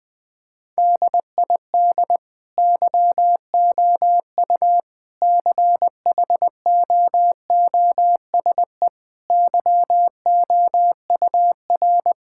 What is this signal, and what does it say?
Morse code, heard as a keyed tone, about 19 wpm, 700 Hz. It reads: DID YOU CHOOSE YOUR
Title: Sentences from Top 1000 Words - Review Part 6 - 20wpm